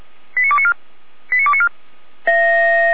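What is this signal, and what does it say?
Five-tone selective call (ZVEI), heard as quick runs of five tones, tones of about 70 ms. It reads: 89294 89294
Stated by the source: ZVEI five-tone call received on an SDR, logged by Feuerwehr Orlamünde.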